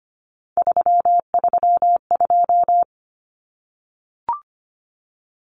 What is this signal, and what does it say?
Morse code, heard as a keyed tone, about 25 wpm, 700 Hz. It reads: 332
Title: Common Cut Numbers - 3-Digit 25wpm